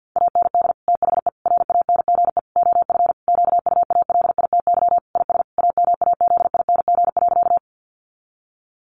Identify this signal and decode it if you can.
Morse code, heard as a keyed tone, about 50 wpm, 700 Hz. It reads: AND THE RANGE OF QUALITY IS AMAZING.
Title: Sentences from Top 1500 Words 50wpm